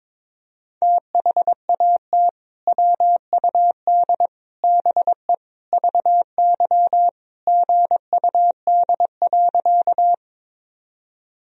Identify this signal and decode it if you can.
Morse code, heard as a keyed tone, about 22 wpm, 700 Hz. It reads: THAT WUD BE VY GUD.
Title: Sentences from Top 200 Words and Abbreviations 22wpm